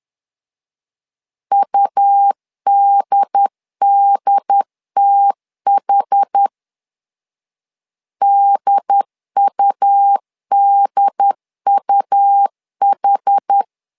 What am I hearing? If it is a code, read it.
Morse code, heard as a keyed tone, about 10 wpm, 800 Hz. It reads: UDDTH DUDUH
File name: LEZIONI / 2 UHDN / 2 UHD 10wpm 1826